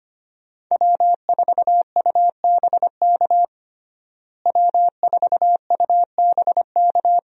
Morse code, heard as a keyed tone, about 25 wpm, 700 Hz. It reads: W4UBK W4UBK
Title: USA Call Signs - 5 Characters - Repeated 3x - 25wpm